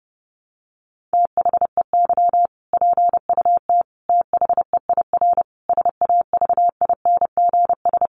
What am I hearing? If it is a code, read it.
Morse code, heard as a keyed tone, about 30 wpm, 700 Hz. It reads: THEY PUT THEIR SAVINGS
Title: Sentences from Top 1200 Words 30wpm